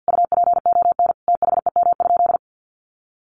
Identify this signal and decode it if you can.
Morse code, heard as a keyed tone, about 48 wpm, 700 Hz. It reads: UPON THEM?
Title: Sentences from Top 1300 Words 50wpm